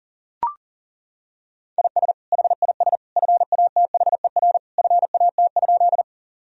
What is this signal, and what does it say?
Morse code, heard as a keyed tone, about 37 wpm, 700 Hz. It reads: IS HIS FATHER FAT?
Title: Sentences from Top 1000 Words - Review Part 4 - 40wpm